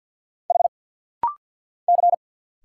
Morse code, heard as a keyed tone, about 50 wpm, 690 Hz. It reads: H C
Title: Single Letters - Rapid-Fire 50wpm